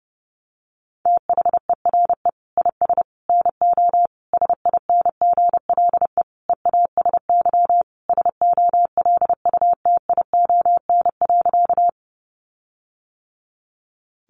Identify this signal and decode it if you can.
Morse code, heard as a keyed tone, about 30 wpm, 700 Hz. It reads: THERE IS NO SINGLE EASY SOLUTION.
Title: Sentences from Top 1300 Words 30wpm